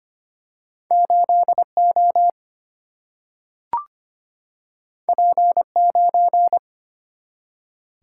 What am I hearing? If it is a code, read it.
Morse code, heard as a keyed tone, about 25 wpm, 700 Hz. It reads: 8 O P 9
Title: Single-Letter Single-Number 25wpm